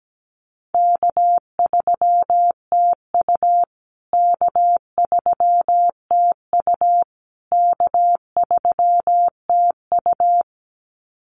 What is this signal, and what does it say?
Morse code, heard as a keyed tone, about 17 wpm, 700 Hz. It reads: K3TU K3TU K3TU
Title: USA Call Signs Repeated 3x 17wpm